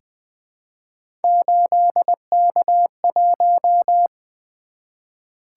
Morse code, heard as a keyed tone, about 20 wpm, 700 Hz. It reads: 8K1
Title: Three Letter-Number Combinations - 20wpm